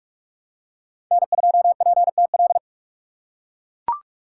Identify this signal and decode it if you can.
Morse code, heard as a keyed tone, about 42 wpm, 700 Hz. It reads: N2PTL
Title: USA Call Signs - 5 Characters - 4x Character Spacing - 45wpm